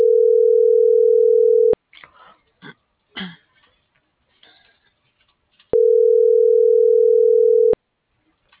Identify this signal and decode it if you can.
Ringback tone (North America), 440 + 480 Hz.